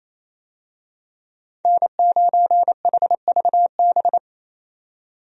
Morse code, heard as a keyed tone, about 28 wpm, 700 Hz. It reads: N9HVB
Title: USA Call Signs - 5 Characters - 28wpm